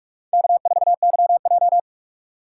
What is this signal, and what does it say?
Morse code, heard as a keyed tone, about 46 wpm, 700 Hz. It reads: K4YJ
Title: USA Call Signs - 4 Characters - 4x Character Spacing - 45wpm